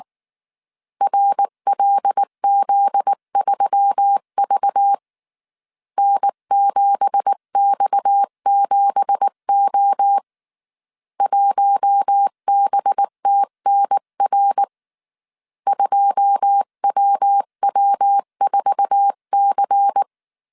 Morse code, heard as a keyed tone, about 19 wpm, 800 Hz. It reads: RLZ3V N7X7O 1BTNR 2WW4C
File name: CW / DET / 1 9 1500C S B 18wpm 2551